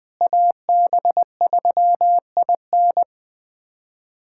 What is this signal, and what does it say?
Morse code, heard as a keyed tone, about 20 wpm, 700 Hz. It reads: AB3IN
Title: USA Callsign DE Callsign - 5 Characters - 20wpm